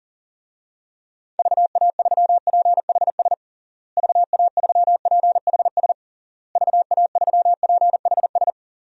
Morse code, heard as a keyed tone, about 40 wpm, 700 Hz. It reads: VA3PHS VA3PHS VA3PHS
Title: DXCC-100 Call Signs - 6 Characters - Repeated 3x - 40wpm